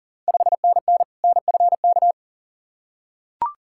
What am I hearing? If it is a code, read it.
Morse code, heard as a keyed tone, about 40 wpm, 690 Hz. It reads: 5NN NFK
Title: Ontario QSO Party Exchange 40wpm